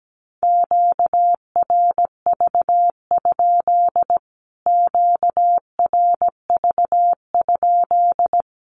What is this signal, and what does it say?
Morse code, heard as a keyed tone, about 17 wpm, 700 Hz. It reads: QRV? QRV?
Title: Q-Codes Repeated 3x 17wpm